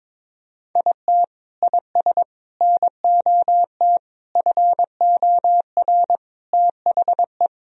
Morse code, heard as a keyed tone, about 22 wpm, 700 Hz. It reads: IT IS NOT FOR THE
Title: Sentences from Top 1000 Words - Review Part 1 - 22wpm